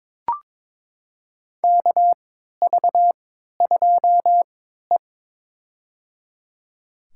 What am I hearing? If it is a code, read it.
Morse code, heard as a keyed tone, about 22 wpm, spaced as for 12 wpm, 700 Hz. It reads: KV2E
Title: USA Call Signs - 3x Character Spacing - 22wpm